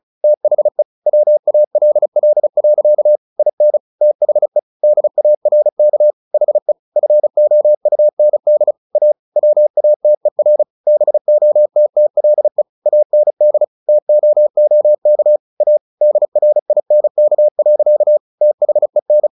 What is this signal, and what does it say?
Morse code, heard as a keyed tone, about 34 wpm, 600 Hz. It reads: THE WALL. IN THE DARK HE FOUND A WATER BOTTLE AND TOOK A DRINK. THEN
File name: the old man and the sea 2 35wpm